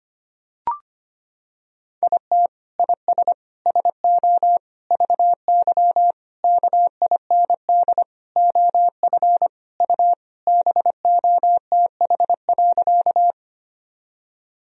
Morse code, heard as a keyed tone, about 25 wpm, 700 Hz. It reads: IT IS SO VY KIND OF U BOTH.
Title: Sentences from Top 300 Words and Abbreviations 25wpm